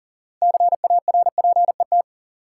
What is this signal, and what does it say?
Morse code, heard as a keyed tone, about 40 wpm, 700 Hz. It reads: CARPET
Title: Six Letter Words - 40wpm